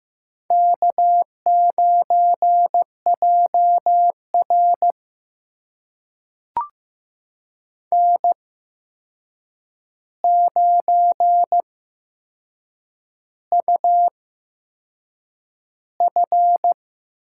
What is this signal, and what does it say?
Morse code, heard as a keyed tone, about 15 wpm, spaced as for 6 wpm, 700 Hz. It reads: K9JR N9UF
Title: USA Call Signs - 4 Characters - 8x Character Spacing - 15wpm